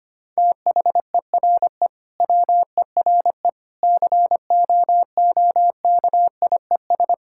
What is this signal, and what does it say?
Morse code, heard as a keyed tone, about 25 wpm, 700 Hz. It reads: THERE WERE COOKIES
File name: practice / 25wpm / Sentences with 4-Word Phrases - 25wpm